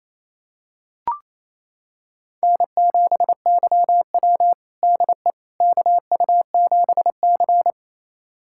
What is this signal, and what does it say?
Morse code, heard as a keyed tone, about 27 wpm, 700 Hz. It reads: N7YW DE KU7C
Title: USA Callsign DE Callsign - 4 Characters - 28wpm